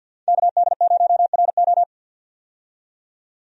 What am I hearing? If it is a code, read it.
Morse code, heard as a keyed tone, about 50 wpm, 700 Hz. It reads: KD0RO
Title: USA Call Signs - 5 Characters - 50wpm